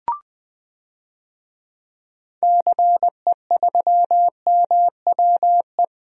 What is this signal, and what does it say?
Morse code, heard as a keyed tone, about 21 wpm, 700 Hz. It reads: CE3MWE
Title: DXCC-100 Call Signs - 6 Characters - 20wpm